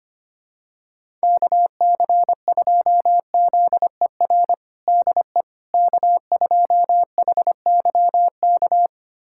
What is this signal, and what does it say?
Morse code, heard as a keyed tone, about 25 wpm, 700 Hz. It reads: KC2ZER DE K2HYK
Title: USA Callsign DE Callsign 25wpm